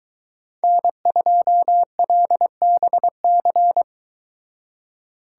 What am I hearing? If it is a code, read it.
Morse code, heard as a keyed tone, about 23 wpm, 700 Hz. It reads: N2LBC